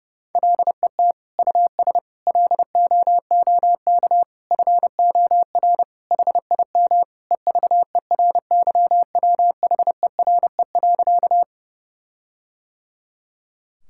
Morse code, heard as a keyed tone, about 32 wpm, 700 Hz. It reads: LET US LOOK FOR HIM EVERYWHERE.
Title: Sentences from Top 1600 Words 30wpm